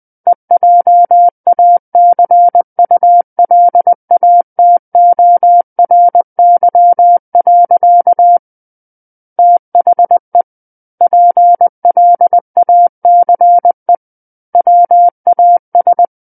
Morse code, heard as a keyed tone, about 20 wpm, 700 Hz. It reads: EJACULATORY. THE PLACE WAS
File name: 20 WPM